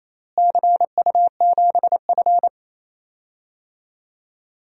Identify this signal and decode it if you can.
Morse code, heard as a keyed tone, about 28 wpm, 700 Hz. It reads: CU7F
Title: DXCC-100 Call Signs - 4 Characters - Repeated 2x - 28wpm